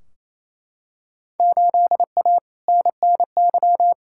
Morse code, heard as a keyed tone, about 28 wpm, 700 Hz. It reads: 8A NNY
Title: Field Day Exchanges Encore - 28wpm